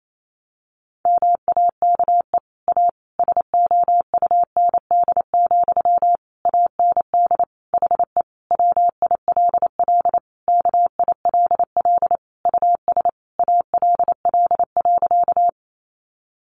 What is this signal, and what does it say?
Morse code, heard as a keyed tone, about 28 wpm, 700 Hz. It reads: MAKE A SOUND, AND HE WILL KILL US ALL.
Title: Sentences from Top 1400 Words 28wpm